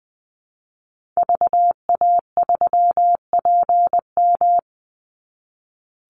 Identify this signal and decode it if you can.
Morse code, heard as a keyed tone, about 20 wpm, 700 Hz. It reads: VA3PM
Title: Call Signs - 20wpm